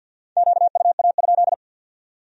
Morse code, heard as a keyed tone, about 62 wpm, 700 Hz. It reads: QUA?
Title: Speed-Racing Q-Codes - 50wpm